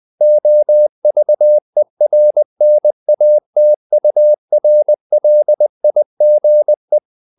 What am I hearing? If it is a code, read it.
Morse code, heard as a keyed tone, about 20 wpm, 600 Hz. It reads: OVERNATURLIGE
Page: Ferie 20 wpm